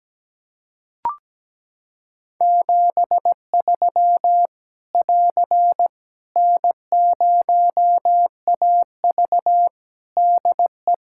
Morse code, heard as a keyed tone, about 17 wpm, 700 Hz. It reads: 73 <AR> N0AV DE
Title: Practice QSO - Exchange 4 - Ending - 17wpm